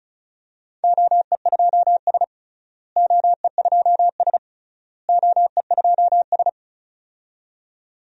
Morse code, heard as a keyed tone, about 36 wpm, 700 Hz. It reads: OE2S OE2S OE2S
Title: DXCC-100 Call Signs - 4 Characters - Repeated 3x - 35wpm